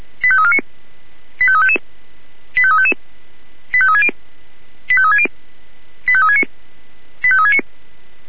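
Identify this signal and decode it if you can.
Five-tone selective call (ZVEI), heard as quick runs of five tones, tones of about 70 ms. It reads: 85349 85355 85355 85360 85360 85369 85369
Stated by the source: Five-tone ZVEI selective call 85342 received on SDR.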